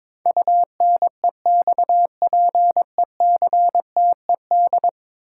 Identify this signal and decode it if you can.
Morse code, heard as a keyed tone, about 22 wpm, 700 Hz. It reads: UNEXPECTED
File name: practice / 22wpm / 10 Letter Words - First 3 Letters Given - 22wpm